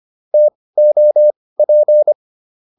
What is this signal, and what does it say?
Morse code, heard as a keyed tone, about 25 wpm, spaced as for 18 wpm, 600 Hz. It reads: TOP